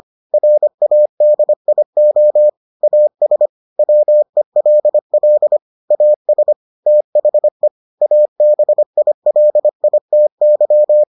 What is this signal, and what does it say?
Morse code, heard as a keyed tone, about 25 wpm, 600 Hz. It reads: RADIO AS WELL AS THE ABILITY